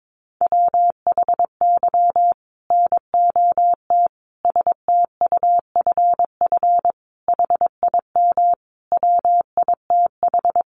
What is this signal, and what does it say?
Morse code, heard as a keyed tone, about 22 wpm, 700 Hz. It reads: WHY NOT STUFF HIM WITH
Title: Sentences from Top 1400 Words 22wpm